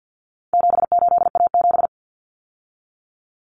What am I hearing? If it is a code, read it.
Morse code, heard as a keyed tone, about 50 wpm, 700 Hz. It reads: 78A7
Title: Common Cut Numbers - 4-Digit 50wpm